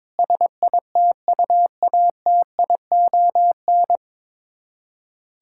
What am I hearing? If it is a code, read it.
Morse code, heard as a keyed tone, about 22 wpm, 700 Hz. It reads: SITUATION
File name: Sets of 9 Words - 22wpm